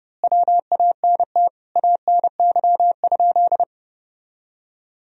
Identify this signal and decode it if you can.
Morse code, heard as a keyed tone, about 30 wpm, 700 Hz. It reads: WANT ANY?
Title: Sentences from Top 300 Words and Abbreviations 30wpm